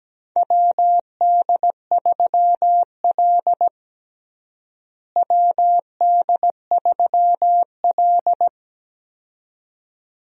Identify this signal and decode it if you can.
Morse code, heard as a keyed tone, about 17 wpm, 700 Hz. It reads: WD3L WD3L
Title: USA Call Signs - 4 Characters - Repeated 2x - 17wpm